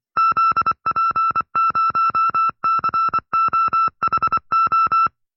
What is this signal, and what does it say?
Morse code, heard as a keyed tone, about 25 wpm, 1330 Hz. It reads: ZP0COHO